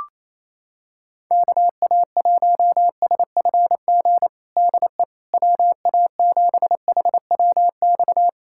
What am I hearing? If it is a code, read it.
Morse code, heard as a keyed tone, about 28 wpm, 700 Hz. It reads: KA1SFG DE WA7HWX
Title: USA Callsign DE Callsign 28wpm